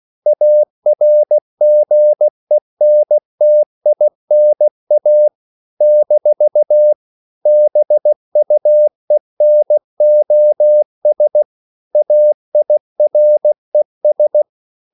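Morse code, heard as a keyed tone, about 16 wpm, 600 Hz. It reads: ARGENTINA - BUENOS AIRES
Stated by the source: CW practice of national capitals at 16 words per minute, from ON6ZQ.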